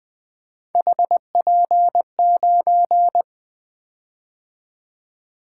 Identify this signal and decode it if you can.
Morse code, heard as a keyed tone, about 20 wpm, 700 Hz. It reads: HP9
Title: Three Letter-Number Combinations - 20wpm